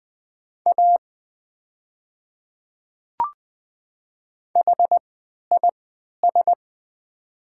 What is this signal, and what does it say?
Morse code, heard as a keyed tone, about 20 wpm, spaced as for 11 wpm, 700 Hz. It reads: A HIS
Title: 3x Character Spacing - Top 100 Words - 20wpm